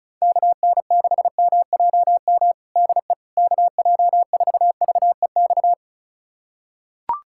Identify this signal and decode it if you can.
Morse code, heard as a keyed tone, about 34 wpm, 690 Hz. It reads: KN6MJM DE KJ4VEX
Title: USA Callsign DE Callsign - 6 Characters - 35wpm